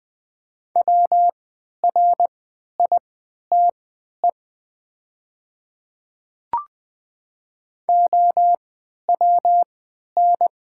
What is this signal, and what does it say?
Morse code, heard as a keyed tone, about 20 wpm, spaced as for 11 wpm, 700 Hz. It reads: WRITE OWN
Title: 3x Character Spacing - Top 300 Words - 20wpm